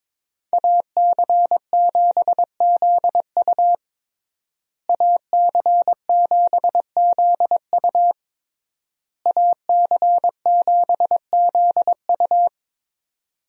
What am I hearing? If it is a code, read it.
Morse code, heard as a keyed tone, about 22 wpm, 700 Hz. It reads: AC7ZU AC7ZU AC7ZU
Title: USA Call Signs - 5 Characters - Repeated 3x - 22wpm